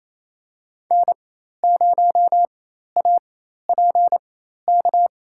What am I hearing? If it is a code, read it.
Morse code, heard as a keyed tone, about 27 wpm, spaced as for 13 wpm, 700 Hz. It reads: N0APK
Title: USA Call Signs - 5 Characters - 4x Character Spacing - 28wpm